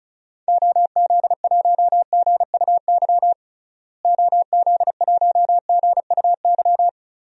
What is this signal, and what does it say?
Morse code, heard as a keyed tone, about 36 wpm, 700 Hz. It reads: OZ1GUY OZ1GUY
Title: DXCC-100 Call Signs - 6 Characters - Repeated 3x - 35wpm